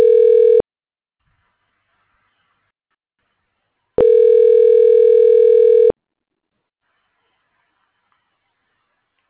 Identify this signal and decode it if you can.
Ringback tone (North America), 440 + 480 Hz.